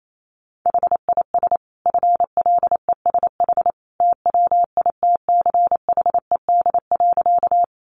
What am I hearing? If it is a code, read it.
Morse code, heard as a keyed tone, about 28 wpm, 700 Hz. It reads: HIS FLESH TWITCHED.
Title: A Sound Of Thunder 28wpm